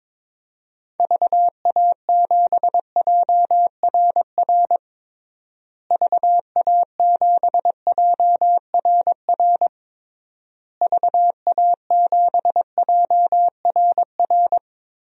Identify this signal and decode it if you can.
Morse code, heard as a keyed tone, about 22 wpm, 700 Hz. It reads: VA7JRR VA7JRR VA7JRR
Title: DXCC-100 Call Signs - 6 Characters - Repeated 3x - 22wpm